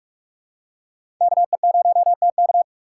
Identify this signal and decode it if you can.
Morse code, heard as a keyed tone, about 46 wpm, 700 Hz. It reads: KE0TK
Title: USA Call Signs - 4x Character Spacing - 45wpm